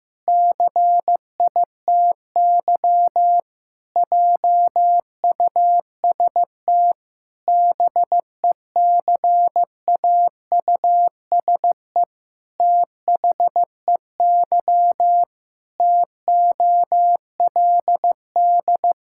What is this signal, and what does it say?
Morse code, heard as a keyed tone, about 15 wpm, 700 Hz. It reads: CITY JUST BECAUSE THEY TOLD
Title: Pangram Challenge v2 - 15wpm